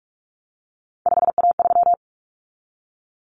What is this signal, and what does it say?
Morse code, heard as a keyed tone, about 44 wpm, 700 Hz. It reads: 5A3
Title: Common Cut Numbers - 3-Digit 45wpm